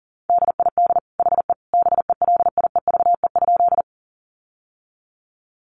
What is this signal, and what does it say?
Morse code, heard as a keyed tone, about 40 wpm, 700 Hz. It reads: DID HE BELIEVE?
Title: Sentences from Top 1000 Words 40wpm